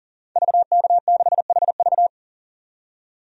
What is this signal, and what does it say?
Morse code, heard as a keyed tone, about 40 wpm, 700 Hz. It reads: VK6HV
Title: DXCC-100 Call Signs - 5 Characters - 40wpm